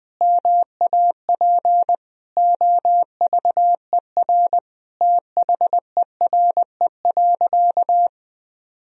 Morse code, heard as a keyed tone, about 21 wpm, 700 Hz. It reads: MAP OVER THERE.
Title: Sentences from Top 1000 Words - Review Part 2 - 20wpm